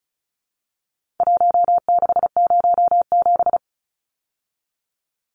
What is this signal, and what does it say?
Morse code, heard as a keyed tone, about 34 wpm, 700 Hz. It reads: 1607
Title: Numbers - Contest Style - 4-Digit 35wpm